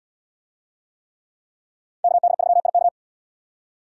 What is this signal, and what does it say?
Morse code, heard as a keyed tone, about 77 wpm, 700 Hz. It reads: KD3IK